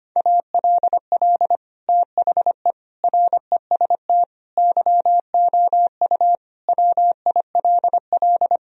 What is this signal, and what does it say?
Morse code, heard as a keyed tone, about 25 wpm, 700 Hz. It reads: ALL THE REST YOU WILL
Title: Sentences from Top 1000 Words - Review Part 4 - 25wpm